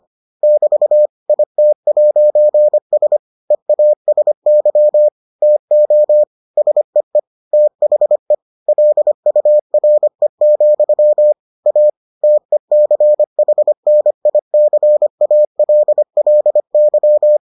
Morse code, heard as a keyed tone, about 25 wpm, 600 Hz. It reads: <BT> IT'S EASY TO SEE THE LURE, A TECHNICALLY